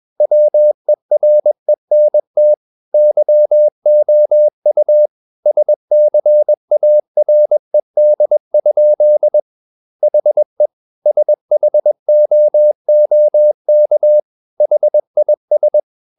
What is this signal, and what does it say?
Morse code, heard as a keyed tone, about 21 wpm, 600 Hz. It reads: WERENT YOU SCARED? HE SHOOK HIS
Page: CW med Gnister